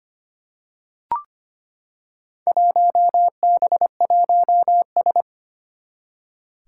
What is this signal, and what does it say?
Morse code, heard as a keyed tone, about 25 wpm, 700 Hz. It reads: 1B1S
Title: Four Challenging Letter-Number Combinations 25wpm